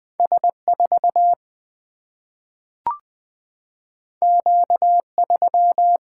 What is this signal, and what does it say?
Morse code, heard as a keyed tone, about 20 wpm, 700 Hz. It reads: S 4 Q 3
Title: Single-Letter Single-Number 20wpm